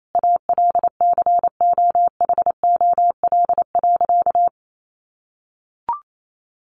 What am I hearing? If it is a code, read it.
Morse code, heard as a keyed tone, about 28 wpm, 700 Hz. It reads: ALCOHOL.
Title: Sentences from Top 1400 Words 28wpm